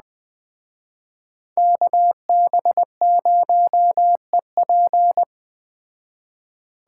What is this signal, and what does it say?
Morse code, heard as a keyed tone, about 20 wpm, 700 Hz. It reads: KB0EP
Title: Speed Racing - USA 2x2 Call Signs 17wpm